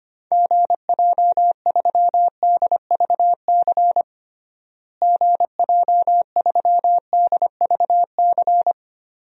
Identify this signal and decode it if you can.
Morse code, heard as a keyed tone, about 25 wpm, 700 Hz. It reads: GJ3DVC GJ3DVC
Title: DXCC-100 Call Signs - 6 Characters - Repeated 2x - 25wpm